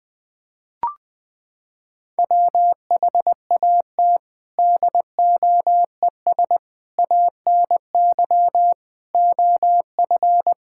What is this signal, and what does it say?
Morse code, heard as a keyed tone, about 21 wpm, 700 Hz. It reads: WHAT DOES ANY OF
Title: Sentences with 9-Word Phrases - 20wpm